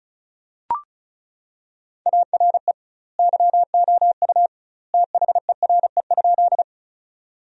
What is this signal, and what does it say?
Morse code, heard as a keyed tone, about 34 wpm, 700 Hz. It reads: ARE YOU THERE?
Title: 2x Word-Spacing - Sentences from Top 100 Words - 35wpm